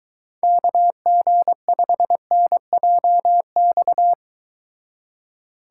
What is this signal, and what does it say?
Morse code, heard as a keyed tone, about 23 wpm, 700 Hz. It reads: KG5NJX